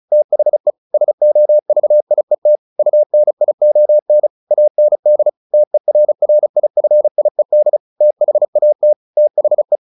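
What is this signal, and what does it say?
Morse code, heard as a keyed tone, about 34 wpm, 600 Hz. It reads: THE SOVIET UNION AND TERRIFIED THAT THE
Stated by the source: Morse practice at 35 wpm from CW med Gnister, sending Ken Follett's novel Never in English.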